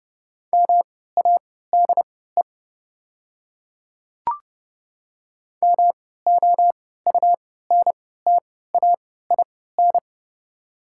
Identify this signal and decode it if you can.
Morse code, heard as a keyed tone, about 28 wpm, spaced as for 17 wpm, 690 Hz. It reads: MADE MOUNTAIN
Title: 3x Character Spacing - Top 300 Words - 30wpm